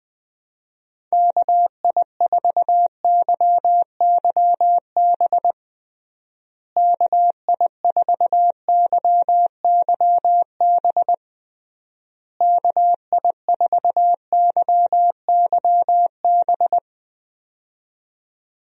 Morse code, heard as a keyed tone, about 20 wpm, 700 Hz. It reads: KI4YYB KI4YYB KI4YYB
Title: USA Call Signs - 6 Characters - Repeated 3x - 20wpm